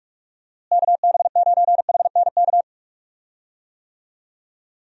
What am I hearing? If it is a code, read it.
Morse code, heard as a keyed tone, about 46 wpm, 700 Hz. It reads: KB9HNK
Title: USA Call Signs - 6 Characters - Repeated 3x - 45wpm